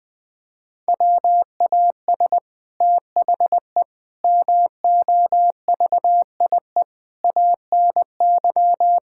Morse code, heard as a keyed tone, about 21 wpm, 700 Hz. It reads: WAS THE MOVIE ANY
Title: Sentences from Top 1000 Words - Review Part 5 - 20wpm